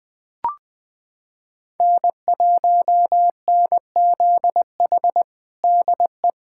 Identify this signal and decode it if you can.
Morse code, heard as a keyed tone, about 21 wpm, 700 Hz. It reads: N1NZH DE
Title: USA Callsign DE Callsign - 5 Characters - 20wpm